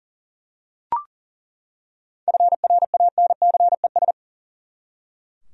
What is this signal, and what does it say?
Morse code, heard as a keyed tone, about 37 wpm, 700 Hz. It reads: FRANCES
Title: USA Common Names - 40wpm